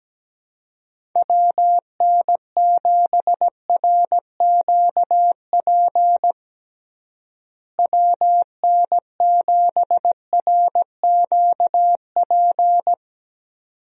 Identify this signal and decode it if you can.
Morse code, heard as a keyed tone, about 17 wpm, 700 Hz. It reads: WN7RQP WN7RQP
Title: USA Call Signs - 6 Characters - Repeated 2x - 17wpm